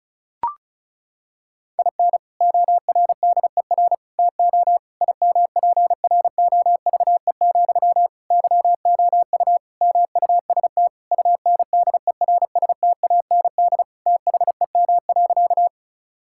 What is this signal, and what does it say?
Morse code, heard as a keyed tone, about 36 wpm, 700 Hz. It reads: IN ORDER TO IMPROVE, YOU MUST UNDERSTAND THEM.
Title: Sentences from Top 1000 Words - Review Part 5 - 35wpm